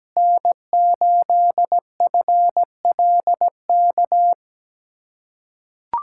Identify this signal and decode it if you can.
Morse code, heard as a keyed tone, about 17 wpm, 700 Hz. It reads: N8FLK
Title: USA Callsign DE Callsign - 5 Characters - 17wpm